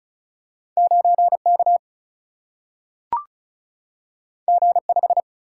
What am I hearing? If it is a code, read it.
Morse code, heard as a keyed tone, about 36 wpm, 700 Hz. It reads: 9K G5
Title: Single-Letter Single-Number 35wpm